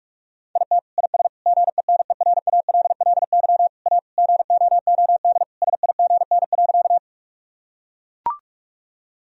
Morse code, heard as a keyed tone, about 41 wpm, 700 Hz. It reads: IT IS GENERALLY A GOOD SIGN.
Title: Sentences from Top 1000 Words - Review Part 4 - 45wpm